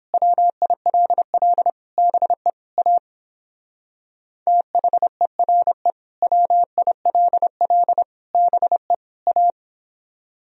Morse code, heard as a keyed tone, about 29 wpm, 700 Hz. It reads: WILL BE A THERE WILL BE A